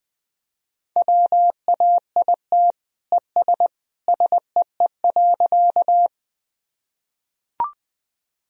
Morse code, heard as a keyed tone, about 20 wpm, 700 Hz. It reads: WAIT ES SEE.
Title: Sentences from Top 500 Words and Abbreviations 20wpm